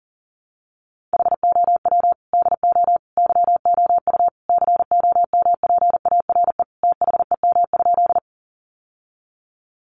Morse code, heard as a keyed tone, about 40 wpm, 700 Hz. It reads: HOW DO YOU COMPARE THEM?
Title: Sentences from Top 1300 Words 40wpm